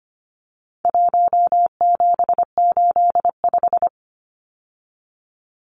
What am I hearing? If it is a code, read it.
Morse code, heard as a keyed tone, about 25 wpm, 700 Hz. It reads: 1785
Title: Numbers - 4-Digit 25wpm